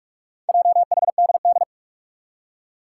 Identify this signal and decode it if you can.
Morse code, heard as a keyed tone, about 44 wpm, 700 Hz. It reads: JHDD
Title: Four Challenging Letter-Number Combinations 45wpm